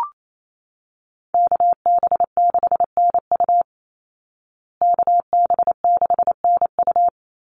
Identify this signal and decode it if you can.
Morse code, heard as a keyed tone, about 28 wpm, 700 Hz. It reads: KB6NU KB6NU
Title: Call Signs Repeated 3x - 28wpm